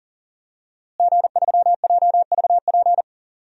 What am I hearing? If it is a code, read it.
Morse code, heard as a keyed tone, about 40 wpm, 700 Hz. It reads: G3JVP
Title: DXCC-100 Call Signs - 5 Characters - Repeated 2x - 40wpm